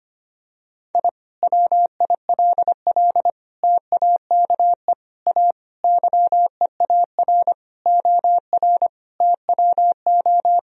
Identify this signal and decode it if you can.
Morse code, heard as a keyed tone, about 25 wpm, 700 Hz. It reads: I WILL TAKE A YEAR OR TWO
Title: Sentences from Top 1000 Words - Review Part 1 - 25wpm